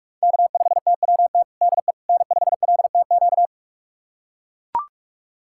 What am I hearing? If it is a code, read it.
Morse code, heard as a keyed tone, about 41 wpm, 700 Hz. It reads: K5TWT DE N5LTQ
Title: USA Callsign DE Callsign - 5 Characters - 45wpm